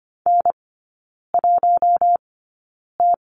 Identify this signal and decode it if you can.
Morse code, heard as a keyed tone, about 25 wpm, spaced as for 9 wpm, 690 Hz. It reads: N1T
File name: Lesson 14e - Number 2 - Three Characters 25wpm